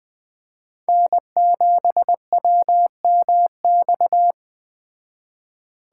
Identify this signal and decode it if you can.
Morse code, heard as a keyed tone, about 20 wpm, 700 Hz. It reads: N7WMX
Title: Speed Racing - USA 1x3 Call Signs 17wpm